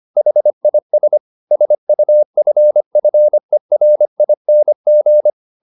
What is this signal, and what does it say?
Morse code, heard as a keyed tone, about 25 wpm, 600 Hz. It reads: HIS SUFFERING
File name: the old man and the sea part 1 25wpm